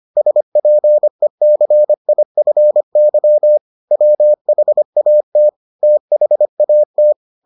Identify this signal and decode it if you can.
Morse code, heard as a keyed tone, about 25 wpm, 600 Hz. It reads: SPECIFY WHAT THAT